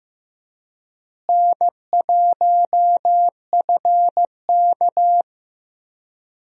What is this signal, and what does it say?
Morse code, heard as a keyed tone, about 15 wpm, 700 Hz. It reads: N1FK
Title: USA Call Signs - 4 Characters - Repeated 3x - 15wpm